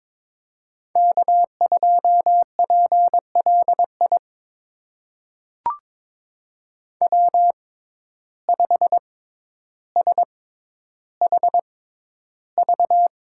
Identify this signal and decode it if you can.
Morse code, heard as a keyed tone, about 22 wpm, spaced as for 11 wpm, 700 Hz. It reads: K2PLI W5SHV
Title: USA Call Signs - 5 Characters - 6x Character Spacing - 22wpm